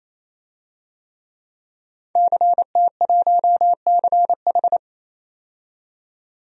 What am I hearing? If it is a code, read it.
Morse code, heard as a keyed tone, about 27 wpm, 700 Hz. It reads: CT1CH
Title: DXCC-100 Call Signs - 5 Characters - 28wpm